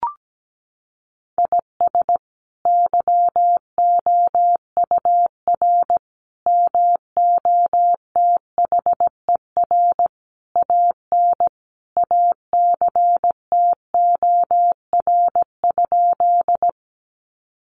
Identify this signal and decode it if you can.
Morse code, heard as a keyed tone, about 17 wpm, 700 Hz. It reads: IS YOUR MOTHER AN ACTOR?
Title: Sentences from Top 1500 Words 17wpm